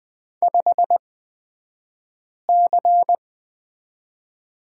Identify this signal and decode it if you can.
Morse code, heard as a keyed tone, about 20 wpm, spaced as for 5 wpm, 700 Hz. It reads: 5C
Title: 249 Lesson 40d - BK - Two Characters 20wpm